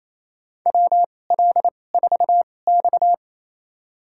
Morse code, heard as a keyed tone, about 27 wpm, spaced as for 20 wpm, 700 Hz. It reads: WL4X
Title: USA Call Signs - 4 Characters - 2x Character Spacing - 28wpm